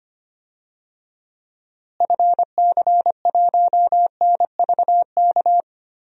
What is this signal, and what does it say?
Morse code, heard as a keyed tone, about 25 wpm, 700 Hz. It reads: FC1NVK
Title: DXCC-100 Call Signs - 6 Characters - Repeated 3x - 25wpm